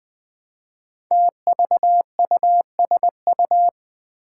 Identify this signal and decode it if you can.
Morse code, heard as a keyed tone, about 20 wpm, 700 Hz. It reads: TVUSU
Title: Five Challenging Letter-Number Combinations 20wpm